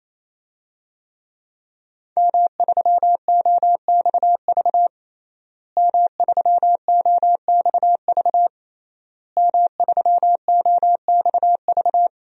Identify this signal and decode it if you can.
Morse code, heard as a keyed tone, about 27 wpm, 700 Hz. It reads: M3OXV M3OXV M3OXV
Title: DXCC-100 Call Signs - 5 Characters - Repeated 3x - 28wpm